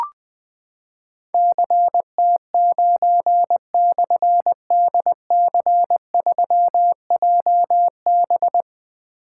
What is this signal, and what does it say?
Morse code, heard as a keyed tone, about 20 wpm, 700 Hz. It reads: CT9/DC3JB
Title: Difficult Call Signs 20wpm